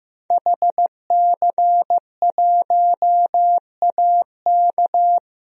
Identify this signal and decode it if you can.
Morse code, heard as a keyed tone, about 15 wpm, 700 Hz. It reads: HC1AK
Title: DXCC-100 Call Signs - 5 Characters - Repeated 2x - 15wpm